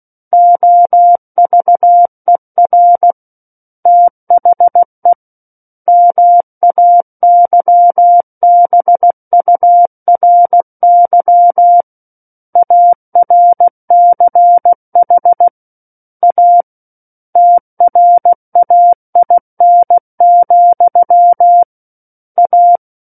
Morse code, heard as a keyed tone, about 16 wpm, 700 Hz. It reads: OVER THE MAYBURY ARCH A TRAIN, A
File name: Book One / 07-Chapter 7 - 16 WPM